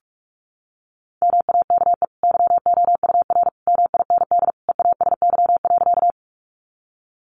Morse code, heard as a keyed tone, about 44 wpm, 700 Hz. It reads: MAKE YOUR MIND EASY.
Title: Sentences from Top 700 Words 45wpm